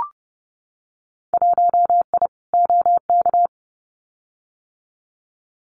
Morse code, heard as a keyed tone, about 30 wpm, 700 Hz. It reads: 1I OK
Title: Winter Field Day Exchanges Encore 30wpm